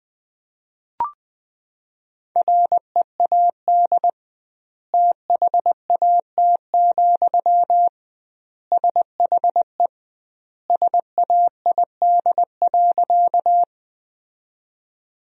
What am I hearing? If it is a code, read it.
Morse code, heard as a keyed tone, about 20 wpm, 700 Hz. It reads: READ THAT, SHE SAID.
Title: 2x Word-Spacing - Sentences from Top 200 Words - 20wpm